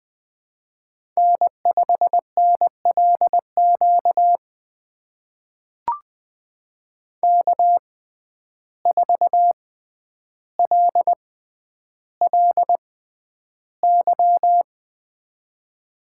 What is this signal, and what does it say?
Morse code, heard as a keyed tone, about 20 wpm, spaced as for 10 wpm, 700 Hz. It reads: N5NLQ K4LLY
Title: USA Call Signs - 5 Characters - 6x Character Spacing - 20wpm